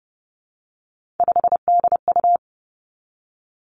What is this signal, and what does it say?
Morse code, heard as a keyed tone, about 30 wpm, 700 Hz. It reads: 5DU
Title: Three Challenging Letter-Number Combinations 30wpm